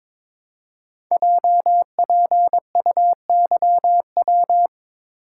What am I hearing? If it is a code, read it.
Morse code, heard as a keyed tone, about 23 wpm, 700 Hz. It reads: JPUYW
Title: Code Groups - 5 Letters - 22wpm